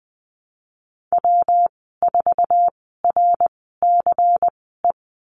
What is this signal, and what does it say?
Morse code, heard as a keyed tone, about 20 wpm, spaced as for 15 wpm, 700 Hz. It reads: W4RCE
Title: 186 Lesson 30g - Letter b - Call Signs 20wpm